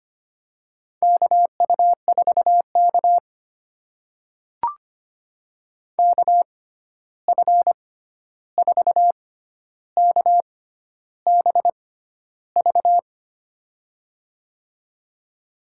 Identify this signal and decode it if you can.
Morse code, heard as a keyed tone, about 25 wpm, spaced as for 11 wpm, 700 Hz. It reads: KU4K KF4KBV
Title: USA Call Signs - 6x Character Spacing - 25wpm